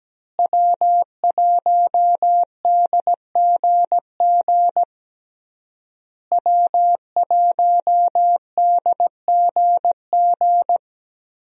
Morse code, heard as a keyed tone, about 17 wpm, 700 Hz. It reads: W1DGG W1DGG
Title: USA Call Signs - 5 Characters - Repeated 3x - 17wpm